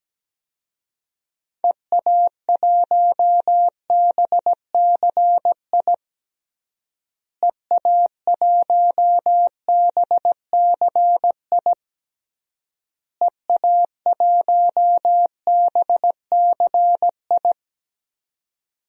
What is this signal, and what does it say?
Morse code, heard as a keyed tone, about 17 wpm, 700 Hz. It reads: EA1BCI EA1BCI EA1BCI
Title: DXCC-100 Call Signs - 6 Characters - Repeated 3x - 17wpm